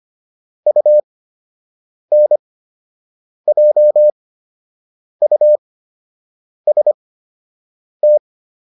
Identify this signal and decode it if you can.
Morse code, heard as a keyed tone, about 25 wpm, spaced as for 7 wpm, 600 Hz. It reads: UNJUST